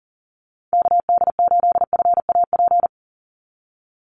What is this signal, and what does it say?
Morse code, heard as a keyed tone, about 40 wpm, 700 Hz. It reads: KD8FAP
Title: USA Call Signs 40wpm